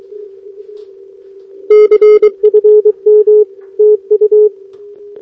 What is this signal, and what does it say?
Morse code, heard as a keyed tone, about 23 wpm, 420 Hz. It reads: CFM TU
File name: OK QRP 2026 / OKQRP2026